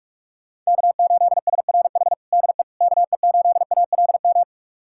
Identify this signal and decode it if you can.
Morse code, heard as a keyed tone, about 44 wpm, 700 Hz. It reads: K8SRH DE KE8ALM
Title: USA Callsign DE Callsign 45wpm